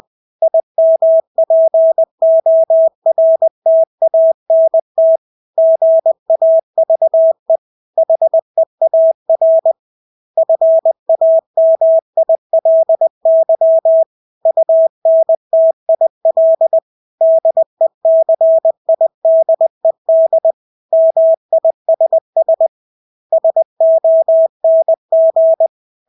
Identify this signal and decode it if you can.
Morse code, heard as a keyed tone, about 20 wpm, 650 Hz. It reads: IMPORTANT GAVE HEAR FAMILY UNTIL DECIDED MISS SONG
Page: Engelske ord 20 wpm